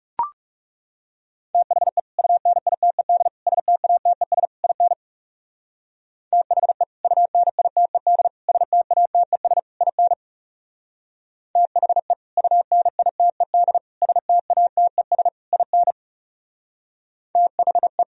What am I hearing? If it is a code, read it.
Morse code, heard as a keyed tone, about 38 wpm, 690 Hz. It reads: THE UNITED STATES IN THE UNITED STATES IN THE UNITED STATES IN THE